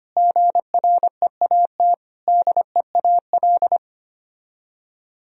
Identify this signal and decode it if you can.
Morse code, heard as a keyed tone, about 32 wpm, 700 Hz. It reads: GREAT DEAL